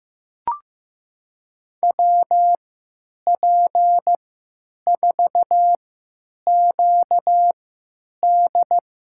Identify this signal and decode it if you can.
Morse code, heard as a keyed tone, about 15 wpm, spaced as for 8 wpm, 700 Hz. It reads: WP4QD
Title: USA Call Signs - 3x Character Spacing - 15wpm